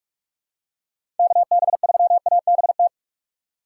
Morse code, heard as a keyed tone, about 46 wpm, 700 Hz. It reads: KB3ABT
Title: USA Call Signs - 6 Characters - 2x Character Spacing - 45wpm